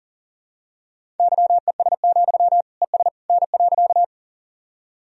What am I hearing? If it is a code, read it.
Morse code, heard as a keyed tone, about 40 wpm, 700 Hz. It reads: YES, ES N.
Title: Sentences from Top 500 Words and Abbreviations 40wpm